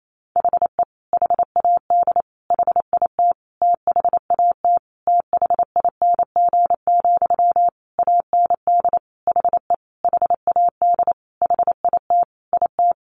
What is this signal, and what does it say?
Morse code, heard as a keyed tone, about 28 wpm, 700 Hz. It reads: HE HAD HIT THAT THING, AND HE HAD HIT IT
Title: Sentences from Top 1200 Words 28wpm